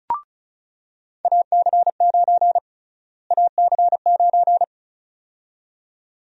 Morse code, heard as a keyed tone, about 36 wpm, 700 Hz. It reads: AC9 AC9
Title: USA Call Sign Prefixes - Repeated 2x - 35wpm